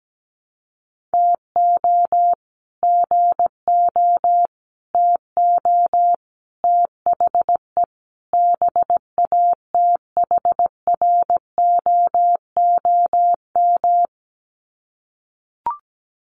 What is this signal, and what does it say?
Morse code, heard as a keyed tone, about 17 wpm, 700 Hz. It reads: TO GO TO THE BATHROOM
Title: Sets of 5 Words Encore 17wpm